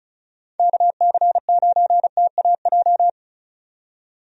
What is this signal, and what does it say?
Morse code, heard as a keyed tone, about 36 wpm, 700 Hz. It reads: KC9TAJ
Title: USA Call Signs - 6 Characters - 35wpm